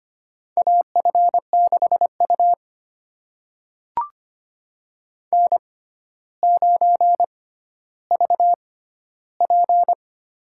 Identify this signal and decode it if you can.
Morse code, heard as a keyed tone, about 25 wpm, spaced as for 13 wpm, 700 Hz. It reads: AF6U N9VP
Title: USA Call Signs - 4 Characters - 6x Character Spacing - 25wpm